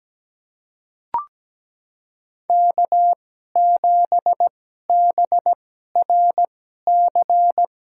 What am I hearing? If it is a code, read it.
Morse code, heard as a keyed tone, about 17 wpm, spaced as for 12 wpm, 700 Hz. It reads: K7BRC
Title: USA Call Signs - 5 Characters - 2x Character Spacing - 17wpm